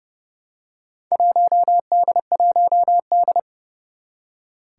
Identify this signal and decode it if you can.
Morse code, heard as a keyed tone, about 30 wpm, 700 Hz. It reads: 1D1D
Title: Four Challenging Letter-Number Combinations 30wpm